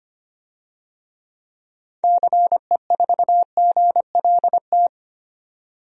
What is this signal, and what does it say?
Morse code, heard as a keyed tone, about 25 wpm, 690 Hz. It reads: CE4GLT
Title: DXCC-100 Call Signs - 6 Characters - 25wpm